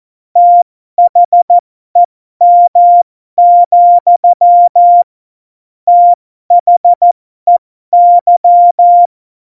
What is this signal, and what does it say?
Morse code, heard as a keyed tone, about 14 wpm, spaced as for 12 wpm, 700 Hz. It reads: THEM, THEY
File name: Morsekurs / Witze